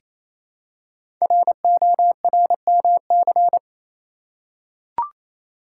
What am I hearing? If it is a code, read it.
Morse code, heard as a keyed tone, about 27 wpm, 700 Hz. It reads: RORMC
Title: Code Groups - 5 Letters - 28wpm